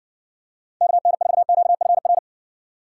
Morse code, heard as a keyed tone, about 56 wpm, 700 Hz. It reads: KN4QFR